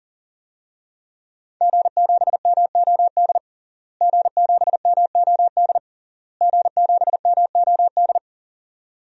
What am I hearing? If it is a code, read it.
Morse code, heard as a keyed tone, about 37 wpm, 700 Hz. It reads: G7MOD G7MOD G7MOD
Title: DXCC-100 Call Signs - 5 Characters - Repeated 3x - 40wpm